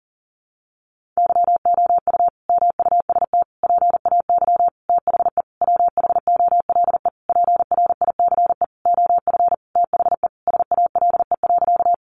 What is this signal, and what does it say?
Morse code, heard as a keyed tone, about 40 wpm, 700 Hz. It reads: YOU MUST PAY THE WHOLE PRICE OF THE SALE.
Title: Sentences from Top 1400 Words 40wpm